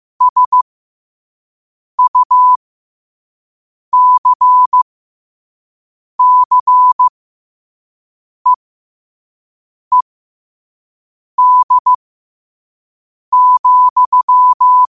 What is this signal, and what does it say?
Morse code, heard as a keyed tone, about 15 wpm, spaced as for 5 wpm, 1000 Hz. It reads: SUCCEED,
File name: qotdfiles / 5wpm